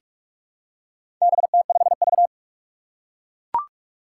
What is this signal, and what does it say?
Morse code, heard as a keyed tone, about 44 wpm, 700 Hz. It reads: BT5V
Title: Four Challenging Letter-Number Combinations 45wpm